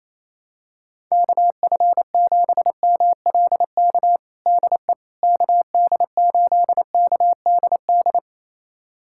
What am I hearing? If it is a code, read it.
Morse code, heard as a keyed tone, about 28 wpm, 700 Hz. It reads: KF7MLK DE KD8KDD
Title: USA Callsign DE Callsign 28wpm